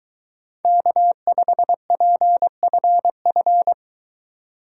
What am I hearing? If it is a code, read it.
Morse code, heard as a keyed tone, about 23 wpm, 700 Hz. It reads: K5PFF